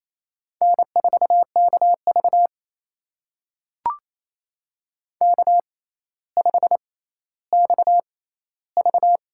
Morse code, heard as a keyed tone, about 27 wpm, spaced as for 14 wpm, 700 Hz. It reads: N4KV K5XV
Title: USA Call Signs - 4 Characters - 6x Character Spacing - 28wpm